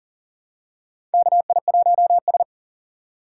Morse code, heard as a keyed tone, about 40 wpm, 700 Hz. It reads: KI1S
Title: USA Call Signs - 4 Characters - 6x Character Spacing - 40wpm